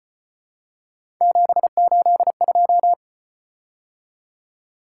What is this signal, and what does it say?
Morse code, heard as a keyed tone, about 32 wpm, 700 Hz. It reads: 782